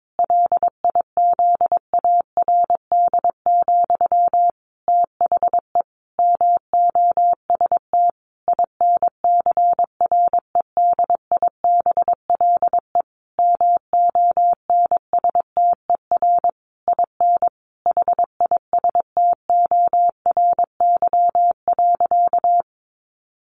Morse code, heard as a keyed tone, about 22 wpm, 700 Hz. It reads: LIZARD, THE MOST INCREDIBLE MONSTER IN HISTORY.
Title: A Sound Of Thunder 22wpm